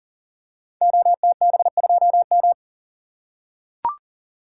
Morse code, heard as a keyed tone, about 40 wpm, 690 Hz. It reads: OTB2M
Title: Code Groups - 5 Letter:Numbers - 40wpm